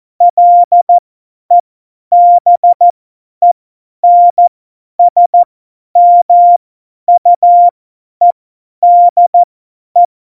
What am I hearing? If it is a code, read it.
Morse code, heard as a keyed tone, about 14 wpm, spaced as for 10 wpm, 700 Hz. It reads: LEBENSMUEDE
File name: Morsekurs / Witze / Witz 17. November 2020, 50 F70